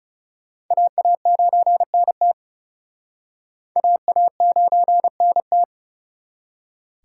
Morse code, heard as a keyed tone, about 37 wpm, 700 Hz. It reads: AA9NT AA9NT
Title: Speed Racing - USA 2x2 Call Signs 30wpm